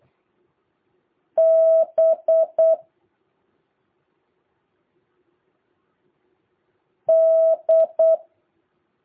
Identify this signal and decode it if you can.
Morse code, heard as a keyed tone, about 8 wpm, 650 Hz. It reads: B D
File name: jh8cbh / 2023cw